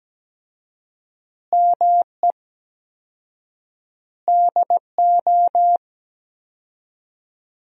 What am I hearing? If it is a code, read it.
Morse code, heard as a keyed tone, about 17 wpm, 700 Hz. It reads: ME DO
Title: 4x Word-Spacing - Sentences from Top 300 Words - 17wpm